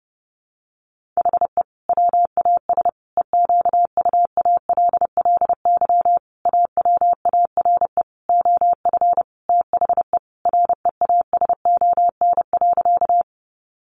Morse code, heard as a keyed tone, about 30 wpm, 700 Hz. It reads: HE WAS EQUALLY AWARE OF THE REASON.
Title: Sentences from Top 1400 Words 30wpm